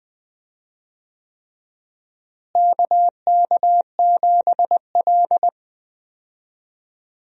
Morse code, heard as a keyed tone, about 20 wpm, 700 Hz. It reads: KK7L
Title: USA Call Signs - 4 Characters - Repeated 3x - 20wpm